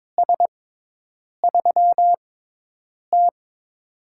Morse code, heard as a keyed tone, about 22 wpm, spaced as for 8 wpm, 700 Hz. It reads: S3T
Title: USA Call Signs - 4 Characters - 6x Character Spacing - 22wpm